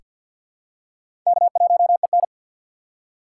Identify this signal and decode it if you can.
Morse code, heard as a keyed tone, about 48 wpm, 700 Hz. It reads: K1EN
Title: USA Call Signs - 6x Character Spacing - 50wpm